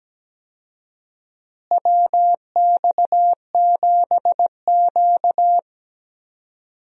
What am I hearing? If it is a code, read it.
Morse code, heard as a keyed tone, about 17 wpm, 700 Hz. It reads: WX7Q
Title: USA Call Signs - 4 Characters - Repeated 3x - 17wpm